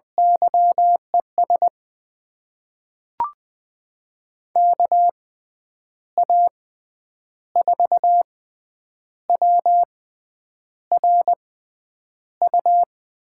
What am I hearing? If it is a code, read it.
Morse code, heard as a keyed tone, about 21 wpm, spaced as for 9 wpm, 700 Hz. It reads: YES KA4WRU
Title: USA Call Signs - 6 Characters - 6x Character Spacing - 20wpm